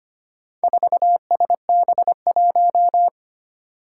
Morse code, heard as a keyed tone, about 25 wpm, 700 Hz. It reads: 4SB1
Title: Four Challenging Letter-Number Combinations 25wpm